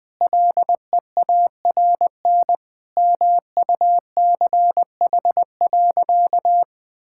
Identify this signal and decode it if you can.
Morse code, heard as a keyed tone, about 20 wpm, 700 Hz. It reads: LEARN MUCH.
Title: Sentences with 2-Word Phrases - Spoken Beginning - 20wpm